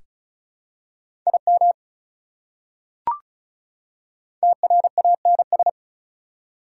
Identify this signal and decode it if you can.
Morse code, heard as a keyed tone, about 34 wpm, 690 Hz. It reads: IM TRANS
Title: Prefixes and Suffixes 35wpm